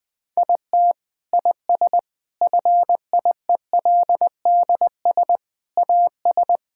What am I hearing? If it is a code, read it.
Morse code, heard as a keyed tone, about 20 wpm, 700 Hz. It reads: IT IS FIELDS AS
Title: Sentences with 7-Word Phrases - Spoken Beginning - 20wpm